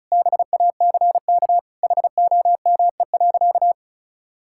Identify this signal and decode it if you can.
Morse code, heard as a keyed tone, about 34 wpm, 700 Hz. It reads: BACK HOME.
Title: Sentences from Top 1000 Words - Review Part 6 - 35wpm